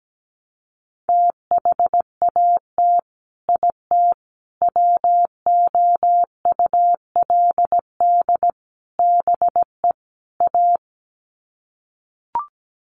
Morse code, heard as a keyed tone, about 17 wpm, 700 Hz. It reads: THAT IT WOULD BE A
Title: Sets of 5 Words Encore2 17wpm